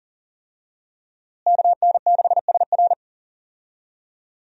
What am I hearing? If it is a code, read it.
Morse code, heard as a keyed tone, about 37 wpm, 700 Hz. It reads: KN6SR
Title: USA Call Signs - 5 Characters - Repeated 2x - 40wpm